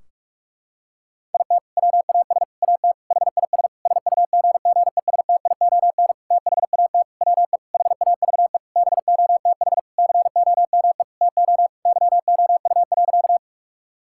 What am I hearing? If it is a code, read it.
Morse code, heard as a keyed tone, about 46 wpm, 700 Hz. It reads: IT WAS AT HIS SUGGESTION THAT WE HAVE BOTH COME TO YOU.
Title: Sentences from Top 1600 Words 45wpm